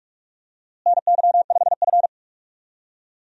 Morse code, heard as a keyed tone, about 41 wpm, 700 Hz. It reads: NY5F
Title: USA Call Signs - 4 Characters - Repeated 3x - 45wpm